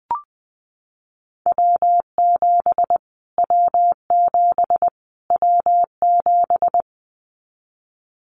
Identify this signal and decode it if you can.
Morse code, heard as a keyed tone, about 20 wpm, 700 Hz. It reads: W7 W7 W7
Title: Callsign Prefixes for Top 25 DXCC - Repeated 3x - 20wpm